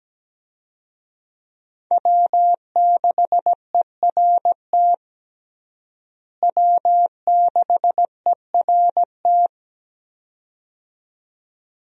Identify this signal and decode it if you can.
Morse code, heard as a keyed tone, about 17 wpm, 700 Hz. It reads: W6ERT W6ERT
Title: USA Call Signs - 5 Characters - Repeated 2x - 17wpm